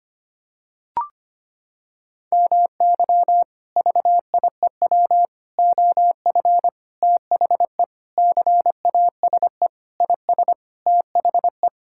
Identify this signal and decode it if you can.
Morse code, heard as a keyed tone, about 25 wpm, 700 Hz. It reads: MY VIEW OF THE CASE IS THE
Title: Sentences from Top 1000 Words - Review Part 6 - 25wpm